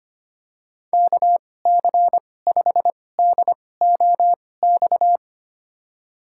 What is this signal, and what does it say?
Morse code, heard as a keyed tone, about 25 wpm, spaced as for 18 wpm, 700 Hz. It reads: KC5DOX
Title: USA Call Signs - 6 Characters - 2x Character Spacing - 25wpm